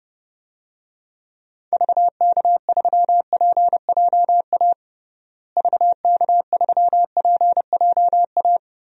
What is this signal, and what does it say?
Morse code, heard as a keyed tone, about 30 wpm, 700 Hz. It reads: VK3PJA VK3PJA
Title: DXCC-100 Call Signs - 6 Characters - Repeated 2x - 30wpm